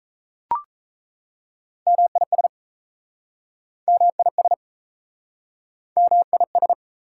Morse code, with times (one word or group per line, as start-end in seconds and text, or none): 1.9-2.5 MIS
3.9-4.5 MIS
6.0-6.7 MIS